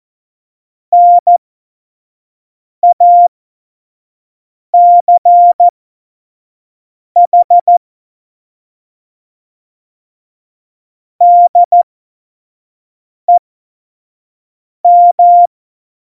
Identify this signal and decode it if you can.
Morse code, heard as a keyed tone, about 14 wpm, spaced as for 5 wpm, 700 Hz. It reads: NACH DEM
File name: Morsekurs / Witze / Witz 15. Dezember, F70